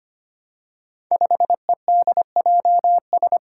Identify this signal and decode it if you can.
Morse code, heard as a keyed tone, about 25 wpm, 700 Hz. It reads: 5EDJS
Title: Five Challenging Letter-Number Combinations 25wpm